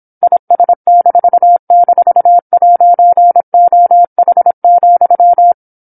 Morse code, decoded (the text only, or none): IS--'OH,